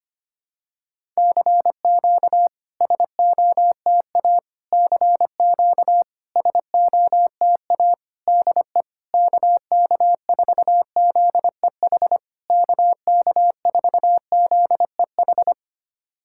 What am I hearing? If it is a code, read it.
Morse code, heard as a keyed tone, about 25 wpm, 690 Hz. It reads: CQ SOTA CQ SOTA DE KK4ZEH KK4ZEH
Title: SOTA QSO Practice 25wpm